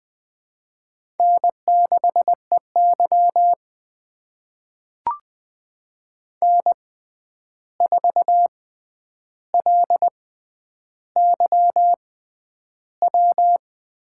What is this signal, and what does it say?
Morse code, heard as a keyed tone, about 20 wpm, spaced as for 10 wpm, 700 Hz. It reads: N6EY N4LYW
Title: USA Call Signs - 6x Character Spacing - 20wpm